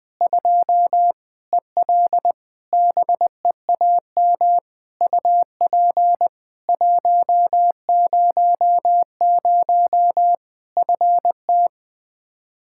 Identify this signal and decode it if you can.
Morse code, heard as a keyed tone, about 20 wpm, 700 Hz. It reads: 2 EL BEAM UP 100 FT
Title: Practice QSO - Exchange 2 - Info - 20wpm